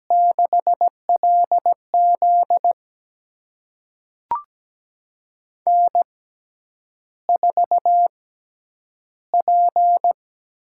Morse code, heard as a keyed tone, about 17 wpm, spaced as for 9 wpm, 700 Hz. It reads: 6LZ N4P
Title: USA Call Signs - 5 Characters - 6x Character Spacing - 17wpm